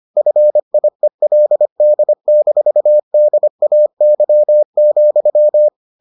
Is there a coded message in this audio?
Yes: Morse code from 0.2 to 5.7 s, heard as a keyed tone, about 25 wpm, 600 Hz. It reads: FIELD-DAY,